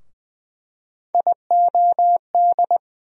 Morse code, heard as a keyed tone, about 20 wpm, 700 Hz. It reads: IOD
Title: USA Call Sign Postfixes - 3 Letters 20wpm